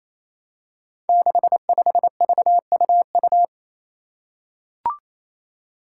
Morse code, heard as a keyed tone, about 28 wpm, 700 Hz. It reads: 65VUU
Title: Five Challenging Letter-Number Combinations 28wpm